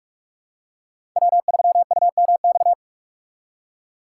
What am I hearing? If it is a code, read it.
Morse code, heard as a keyed tone, about 41 wpm, 700 Hz. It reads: W3UMX
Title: USA Call Signs - 5 Characters - 4x Character Spacing - 45wpm